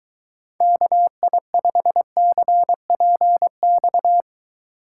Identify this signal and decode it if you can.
Morse code, heard as a keyed tone, about 22 wpm, 700 Hz. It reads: KI5CPX